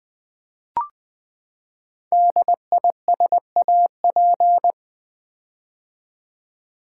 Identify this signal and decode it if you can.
Morse code, heard as a keyed tone, about 20 wpm, 700 Hz. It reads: DISAP
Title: Build A Word - 5 Letters At A Time - 10-Letter Words - 20wpm